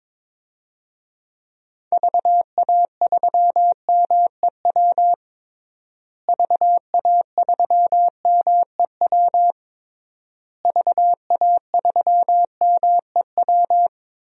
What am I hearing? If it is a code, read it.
Morse code, heard as a keyed tone, about 22 wpm, 700 Hz. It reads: VA3MEW VA3MEW VA3MEW
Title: DXCC-100 Call Signs - 6 Characters - Repeated 3x - 22wpm